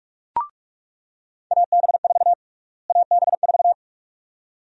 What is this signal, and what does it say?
Morse code, heard as a keyed tone, about 46 wpm, 700 Hz. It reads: AB4 AB4
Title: USA Call Sign Prefixes - Repeated 2x - 45wpm